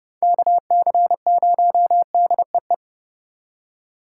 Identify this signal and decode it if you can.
Morse code, heard as a keyed tone, about 37 wpm, 700 Hz. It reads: KC0DEE
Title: Speed Racing - USA 2x3 Call Signs 30wpm